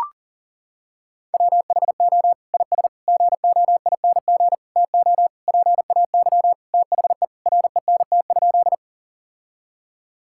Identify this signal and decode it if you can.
Morse code, heard as a keyed tone, about 40 wpm, 700 Hz. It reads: WHO IS GOING TO PAY THE RENT?
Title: Sentences from Top 1000 Words - Review Part 5 - 40wpm